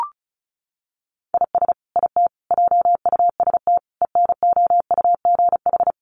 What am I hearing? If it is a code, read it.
Morse code, heard as a keyed tone, about 34 wpm, 700 Hz. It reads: IS IT JUST ENOUGH
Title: Sentences from Top 1500 Words 35wpm